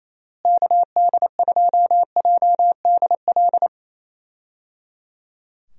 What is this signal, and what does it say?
Morse code, heard as a keyed tone, about 28 wpm, 700 Hz. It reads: KD2JDL
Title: USA Call Signs - 6 Characters - Repeated 3x - 28wpm